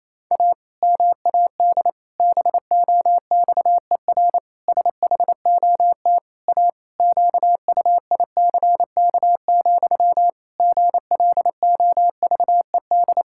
Morse code, heard as a keyed tone, about 28 wpm, 700 Hz. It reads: A MAD BOXER SHOT A QUICK, GLOVED
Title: Pangram Challenge v2 - 28wpm